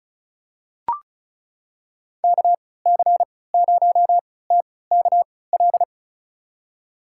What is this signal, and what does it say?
Morse code, heard as a keyed tone, about 36 wpm, spaced as for 20 wpm, 700 Hz. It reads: KC0TKL
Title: USA Call Signs - 6 Characters - 3x Character Spacing - 35wpm